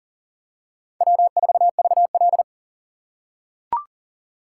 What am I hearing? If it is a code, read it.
Morse code, heard as a keyed tone, about 40 wpm, 700 Hz. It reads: W4VL
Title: USA Call Signs - 4 Characters - 6x Character Spacing - 40wpm